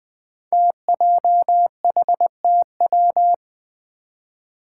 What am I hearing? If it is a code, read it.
Morse code, heard as a keyed tone, about 20 wpm, 700 Hz. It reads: TJHTW
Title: Five Challenging Letter-Number Combinations 20wpm